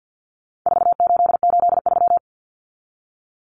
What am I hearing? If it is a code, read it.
Morse code, heard as a keyed tone, about 50 wpm, 700 Hz. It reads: 4883
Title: Numbers - 4-Digit 50wpm